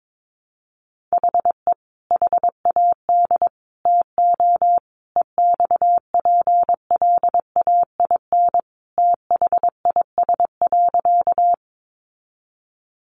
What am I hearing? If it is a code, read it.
Morse code, heard as a keyed tone, about 22 wpm, 700 Hz. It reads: HE HAD TO EXPLAIN THIS.
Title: Sentences from Top 1100 Words 22wpm